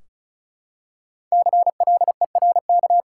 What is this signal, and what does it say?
Morse code, heard as a keyed tone, about 34 wpm, 700 Hz. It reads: CLERK
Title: Five Letter Words - 35wpm